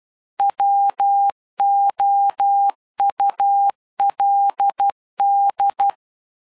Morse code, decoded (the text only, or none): WOULD